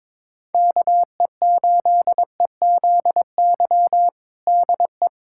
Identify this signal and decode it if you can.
Morse code, heard as a keyed tone, about 22 wpm, 700 Hz. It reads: KE8EZY DE
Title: USA Callsign DE Callsign - 6 Characters - 22wpm